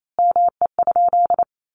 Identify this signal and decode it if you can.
Morse code, heard as a keyed tone, about 28 wpm, 700 Hz. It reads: ME?
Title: Sentences from Top 1400 Words 28wpm